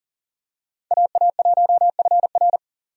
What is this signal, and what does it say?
Morse code, heard as a keyed tone, about 40 wpm, 700 Hz. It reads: AA1FR
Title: USA Call Signs - 5 Characters - 40wpm